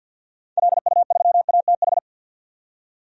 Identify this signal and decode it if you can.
Morse code, heard as a keyed tone, about 47 wpm, 700 Hz. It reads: LU3ATH
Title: DXCC-100 Call Signs - 6 Characters - 50wpm